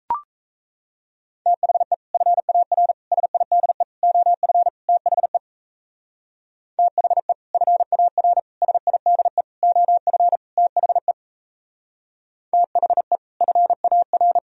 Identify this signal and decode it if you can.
Morse code, heard as a keyed tone, about 34 wpm, 700 Hz. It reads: THE FAR SIDE OF THE THE FAR SIDE OF THE THE FAR
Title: Speed-Racing - Sets of 5 Words 28wpm